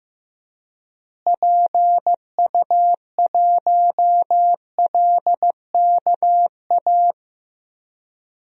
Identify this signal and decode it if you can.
Morse code, heard as a keyed tone, about 15 wpm, 700 Hz. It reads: PU1LKA
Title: DXCC-100 Call Signs - 6 Characters - Repeated 3x - 15wpm